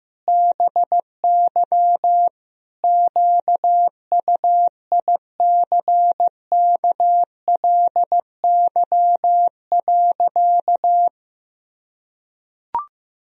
Morse code, heard as a keyed tone, about 15 wpm, 700 Hz. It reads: BY QUICKLY.
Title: Sentences from Top 1000 Words - Review Part 2 - 15wpm